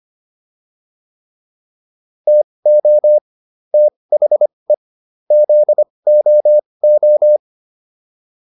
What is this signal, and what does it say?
Morse code, heard as a keyed tone, about 25 wpm, spaced as for 20 wpm, 600 Hz. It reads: TO THE ZOO